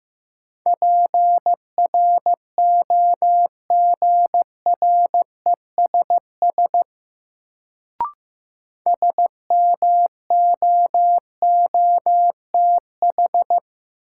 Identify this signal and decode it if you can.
Morse code, heard as a keyed tone, about 15 wpm, 700 Hz. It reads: PROGRESS SMOOTH
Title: Sentences from Top 1400 Words - New Word Drill - 15wpm